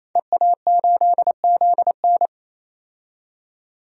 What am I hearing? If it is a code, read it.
Morse code, heard as a keyed tone, about 28 wpm, 700 Hz. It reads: EA8ZN
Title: DXCC-100 Call Signs - 5 Characters - 28wpm